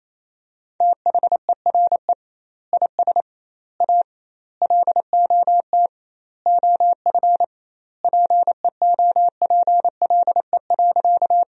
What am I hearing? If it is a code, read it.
Morse code, heard as a keyed tone, about 28 wpm, 700 Hz. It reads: THERE IS A LOT OF PEOPLE.
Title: 2x Word-Spacing - Sentences from Top 500 Words - 28wpm